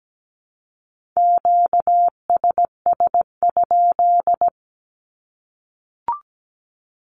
Morse code, heard as a keyed tone, about 17 wpm, 700 Hz. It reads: QSS?
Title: Q-Codes 17wpm